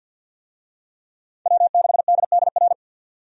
Morse code, heard as a keyed tone, about 52 wpm, 700 Hz. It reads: W6DDR
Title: USA Call Signs - 5 Characters - Repeated 3x - 50wpm